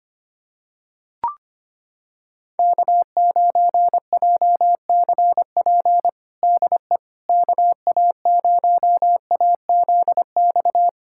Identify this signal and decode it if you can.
Morse code, heard as a keyed tone, about 25 wpm, 700 Hz. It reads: K9JCP DE KA0AZX
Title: USA Callsign DE Callsign 25wpm